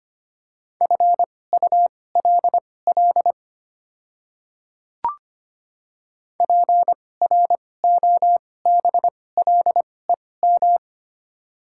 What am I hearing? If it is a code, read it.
Morse code, heard as a keyed tone, about 25 wpm, spaced as for 18 wpm, 700 Hz. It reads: FULL PROBLEM
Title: 2x Character Spacing - Top 500 Words - 25wpm